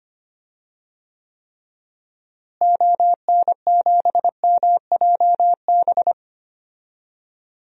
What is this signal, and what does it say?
Morse code, heard as a keyed tone, about 25 wpm, 690 Hz. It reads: ON7MJB
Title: DXCC-100 Call Signs - 6 Characters - Repeated 3x - 25wpm